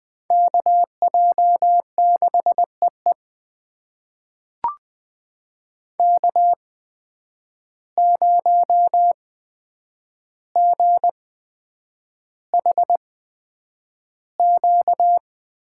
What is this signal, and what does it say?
Morse code, heard as a keyed tone, about 20 wpm, spaced as for 9 wpm, 700 Hz. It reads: KJ6EE K0GHQ
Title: USA Call Signs - 5 Characters - 8x Character Spacing - 20wpm